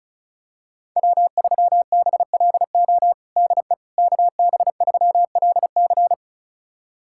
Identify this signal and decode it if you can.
Morse code, heard as a keyed tone, about 36 wpm, 700 Hz. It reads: W3BLO DE KB3LC
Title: USA Callsign DE Callsign - 5 Characters - 35wpm